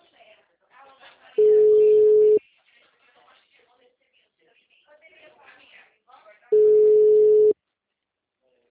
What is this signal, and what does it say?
Ringback tone (Europe, 425 Hz), with speech.